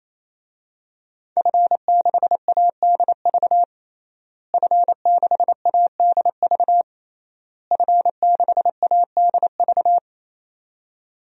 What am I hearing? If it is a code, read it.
Morse code, heard as a keyed tone, about 28 wpm, 700 Hz. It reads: F6ADV F6ADV F6ADV
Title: DXCC-100 Call Signs - 5 Characters - Repeated 3x - 28wpm